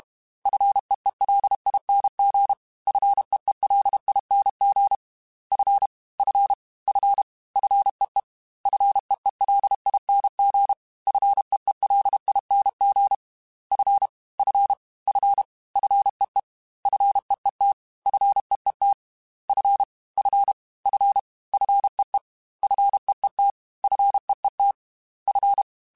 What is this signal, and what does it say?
Morse code, heard as a keyed tone, about 32 wpm, 800 Hz. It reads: FEELING FEELING F F F FEE FEELING FEELING F F F FEE FEET FEET F F F FEE FEET FEET F
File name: Fs / Single F mp3s / Single F Prefixes 3x2 Twice @30